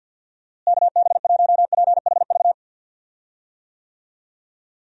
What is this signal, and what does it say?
Morse code, heard as a keyed tone, about 50 wpm, 700 Hz. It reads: KB1PHV
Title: USA Call Signs - 6 Characters - Repeated 2x - 50wpm